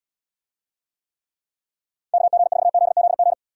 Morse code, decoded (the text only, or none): KG4ZZG